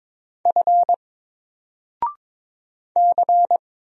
Morse code, heard as a keyed tone, about 22 wpm, 700 Hz. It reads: F C
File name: Single-Letter-Number-22wpm